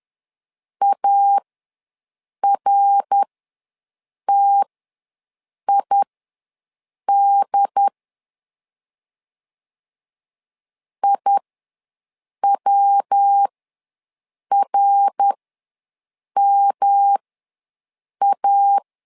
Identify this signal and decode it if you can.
Morse code, heard as a keyed tone, about 11 wpm, spaced as for 6 wpm, 800 Hz. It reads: ARTID IWRMA
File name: LEZIONI / 3 MWRO / 3 MWR ws 10wpm 2151